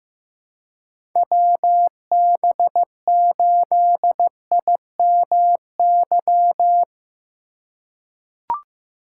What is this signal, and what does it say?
Morse code, heard as a keyed tone, about 15 wpm, 700 Hz. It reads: WB8IMY
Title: K1USN SST QSO Practice - 15wpm